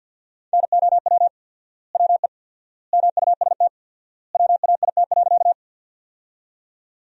Morse code, heard as a keyed tone, about 51 wpm, 700 Hz. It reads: NOW WE MUST WAIT.
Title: 4x Word-Spacing - Sentences from Top 500 Words - 50wpm